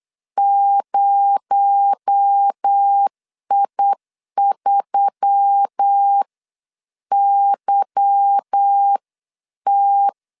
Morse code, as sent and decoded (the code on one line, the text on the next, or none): ----- .. ...-- -.-- -
0I3YT